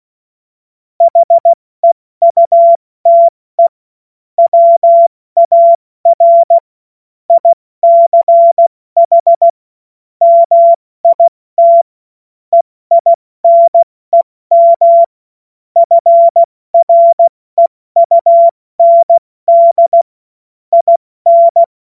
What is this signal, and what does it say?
Morse code, heard as a keyed tone, about 16 wpm, spaced as for 14 wpm, 680 Hz. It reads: HEUTE WAR ICH MIT EINEM FREUND IN
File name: Morsekurs / Witze / Witz 27. Juli 2021, 70 F80